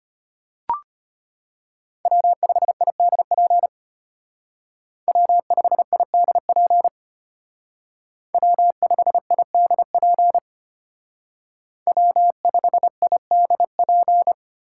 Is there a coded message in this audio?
Yes: Morse code from 2.1 to 14.3 s, heard as a keyed tone, about 32 wpm, 700 Hz. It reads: W5IDP W5IDP W5IDP W5IDP